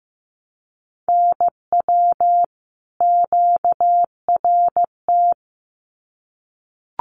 Morse code, heard as a keyed tone, about 15 wpm, 700 Hz. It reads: NW QRT
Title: Common QSO Phrases 15wpm